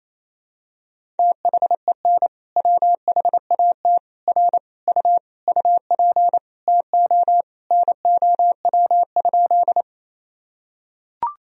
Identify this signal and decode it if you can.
Morse code, heard as a keyed tone, about 28 wpm, 700 Hz. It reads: THEN WHAT R U UP TO NOW?
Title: Sentences from Top 100 Words and Abbreviations 28wpm